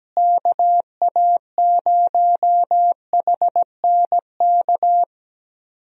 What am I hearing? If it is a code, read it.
Morse code, heard as a keyed tone, about 17 wpm, 700 Hz. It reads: KA0HNK
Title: USA Call Signs - 6 Characters - Repeated 3x - 17wpm